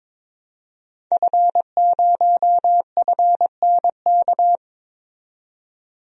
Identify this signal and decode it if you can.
Morse code, heard as a keyed tone, about 21 wpm, 700 Hz. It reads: F0FNK
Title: DXCC-100 Call Signs - 5 Characters - 22wpm